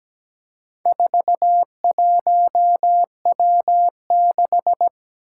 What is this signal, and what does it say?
Morse code, heard as a keyed tone, about 17 wpm, 700 Hz. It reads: 41W6
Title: Four Challenging Letter-Number Combinations 17wpm